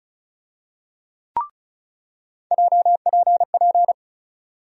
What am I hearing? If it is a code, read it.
Morse code, heard as a keyed tone, about 36 wpm, 700 Hz. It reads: JPP
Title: USA Call Sign Postfixes - 3 Letters 35wpm